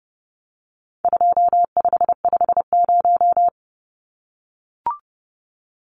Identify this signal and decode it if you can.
Morse code, heard as a keyed tone, about 30 wpm, 700 Hz. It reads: 2550
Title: Numbers - 4-Digit 30wpm